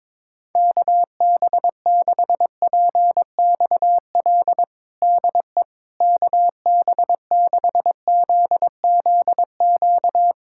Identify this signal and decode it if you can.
Morse code, heard as a keyed tone, about 22 wpm, 700 Hz. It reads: KB6PXL DE KB6ZZQ
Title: USA Callsign DE Callsign - 6 Characters - 22wpm